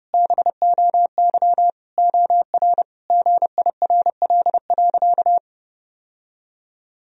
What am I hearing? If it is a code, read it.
Morse code, heard as a keyed tone, about 28 wpm, 700 Hz. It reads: BOY OR GIRL.
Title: Sentences from Top 1000 Words - Review Part 3 - 30wpm